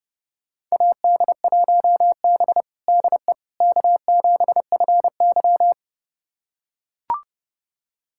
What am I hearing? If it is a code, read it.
Morse code, heard as a keyed tone, about 32 wpm, 700 Hz. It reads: AD1B DE K7FY
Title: USA Callsign DE Callsign - 4 Characters - 30wpm